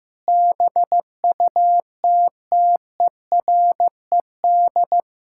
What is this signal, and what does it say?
Morse code, heard as a keyed tone, about 15 wpm, 700 Hz. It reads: BUTTERED
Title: Pangram Challenge v2 - 15wpm